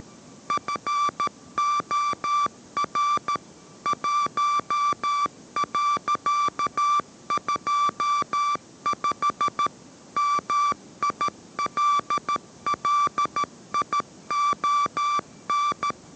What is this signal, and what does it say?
Morse code, heard as a keyed tone, about 13 wpm, 1170 Hz. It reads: FOR 1.25 MILLION